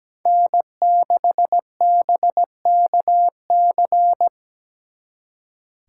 Morse code, heard as a keyed tone, about 20 wpm, 700 Hz. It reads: N6BKC
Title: Speed Racing - USA Call Signs 17wpm